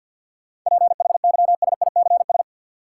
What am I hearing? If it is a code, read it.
Morse code, heard as a keyed tone, about 50 wpm, 700 Hz. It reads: PHYSICS
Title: Sentences from Top 1500 Words - New Word Drill Alt - 50wpm